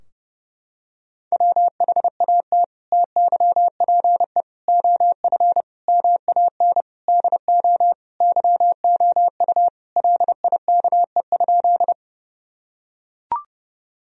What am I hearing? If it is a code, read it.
Morse code, heard as a keyed tone, about 31 wpm, 700 Hz. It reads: WHAT TYPE OF MAN DO YOU LIKE?
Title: Sentences from Top 1000 Words - Review Part 3 - 30wpm